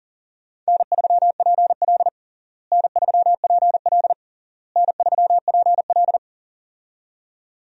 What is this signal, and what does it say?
Morse code, heard as a keyed tone, about 37 wpm, 700 Hz. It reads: N3PL N3PL N3PL
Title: USA Call Signs - 4 Characters - Repeated 3x - 40wpm